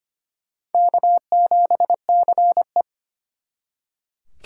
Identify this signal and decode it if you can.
Morse code, heard as a keyed tone, about 32 wpm, 700 Hz. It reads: K7CE